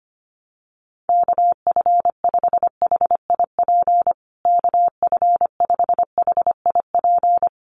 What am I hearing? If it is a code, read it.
Morse code, heard as a keyed tone, about 25 wpm, 700 Hz. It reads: KF5HIP KF5HIP
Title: USA Call Signs Repeated 3x 25wpm